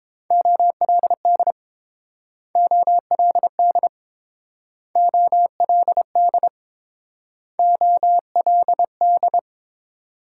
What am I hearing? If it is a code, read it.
Morse code, heard as a keyed tone, about 28 wpm, 700 Hz. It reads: OLD OLD OLD OLD